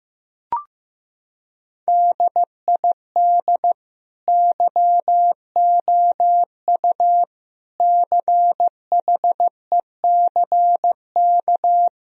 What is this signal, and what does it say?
Morse code, heard as a keyed tone, about 15 wpm, 700 Hz. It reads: DID YOU CHECK
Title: Sentences from Top 1000 Words - Review Part 3 - 15wpm